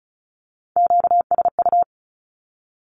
Morse code, heard as a keyed tone, about 34 wpm, 700 Hz. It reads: QSU
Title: Q-Codes 35wpm